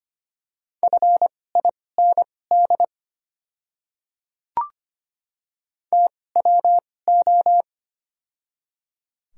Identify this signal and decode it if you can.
Morse code, heard as a keyed tone, about 25 wpm, spaced as for 18 wpm, 690 Hz. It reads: FIND TWO
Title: 2x Character Spacing - Top 300 Words - 25wpm